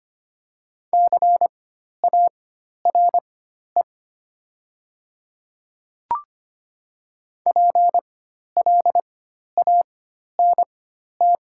Morse code, heard as a keyed tone, about 25 wpm, spaced as for 12 wpm, 700 Hz. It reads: CARE PLANT
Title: 4x Character Spacing - Top 300 Words - 25wpm